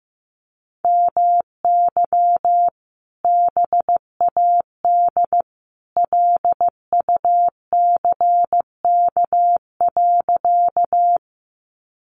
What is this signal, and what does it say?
Morse code, heard as a keyed tone, about 15 wpm, 700 Hz. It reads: MY BAD LUCK.
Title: Sentences from Top 1500 Words 15wpm